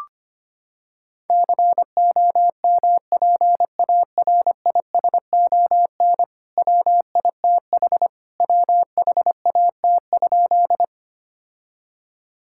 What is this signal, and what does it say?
Morse code, heard as a keyed tone, about 25 wpm, 700 Hz. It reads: COMPARISON WITH WHAT?
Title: Sentences from Top 1600 Words 25wpm